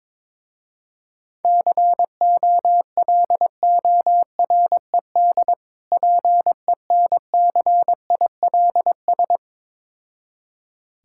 Morse code, heard as a keyed tone, about 22 wpm, 700 Hz. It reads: COLORED PENCILS
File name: Difficult Words with Hints - 22wpm